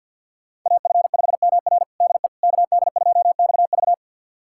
Morse code, heard as a keyed tone, about 52 wpm, 700 Hz. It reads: AF5MR DE KD2XV
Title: USA Callsign DE Callsign - 5 Characters - 50wpm